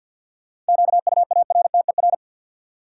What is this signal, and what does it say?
Morse code, heard as a keyed tone, about 53 wpm, 700 Hz. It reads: QUARTER
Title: Sentences from Top 1300 Words - New Word Drill Alt - 50wpm